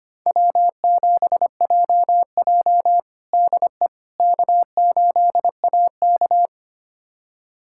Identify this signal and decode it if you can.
Morse code, heard as a keyed tone, about 25 wpm, 700 Hz. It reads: W7JJ DE K8AK
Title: USA Callsign DE Callsign - 4 Characters - 25wpm